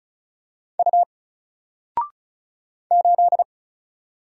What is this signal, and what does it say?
Morse code, heard as a keyed tone, about 36 wpm, 700 Hz. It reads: U 8
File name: Single-Letter-Number-35wpm